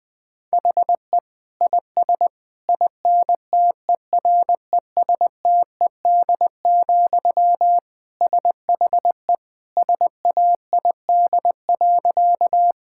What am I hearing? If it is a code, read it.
Morse code, heard as a keyed tone, about 20 wpm, 700 Hz. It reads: HE IS INTERESTED, SHE SAID.
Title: Sentences from Top 1000 Words - Review Part 6 - 20wpm